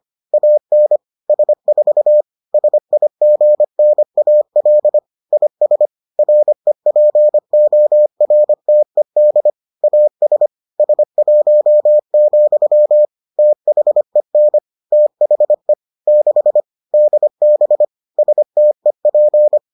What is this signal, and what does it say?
Morse code, heard as a keyed tone, about 25 wpm, 600 Hz. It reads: AN S4 SIGNAL IS REPORTED AS S1, THEN THE 6 DB STEP